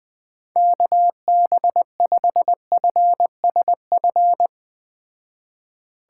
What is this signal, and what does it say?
Morse code, heard as a keyed tone, about 20 wpm, 700 Hz. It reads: KB5FSF